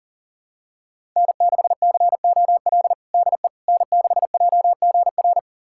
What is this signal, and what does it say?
Morse code, heard as a keyed tone, about 40 wpm, 700 Hz. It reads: N6COL DE N6JGR
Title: USA Callsign DE Callsign - 5 Characters - 40wpm